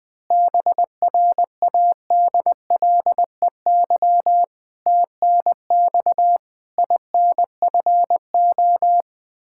Morse code, heard as a keyed tone, about 20 wpm, 700 Hz. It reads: BRADLEY TNX INFO
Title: Practice QSO - Exchange 3 - More Info - 20wpm